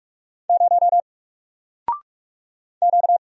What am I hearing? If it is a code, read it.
Morse code, heard as a keyed tone, about 44 wpm, 700 Hz. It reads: 0 Q
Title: Single Letter-Number Alt 45wpm